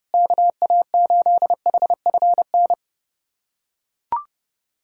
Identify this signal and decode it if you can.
Morse code, heard as a keyed tone, about 30 wpm, 700 Hz. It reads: KA8HFN
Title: USA Callsign DE Callsign - 6 Characters - 30wpm